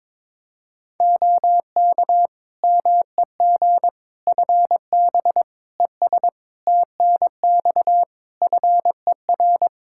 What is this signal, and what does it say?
Morse code, heard as a keyed tone, about 22 wpm, 700 Hz. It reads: OK MEG FB ES TNX FER
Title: Practice QSO - Exchange 2 - Info - 22wpm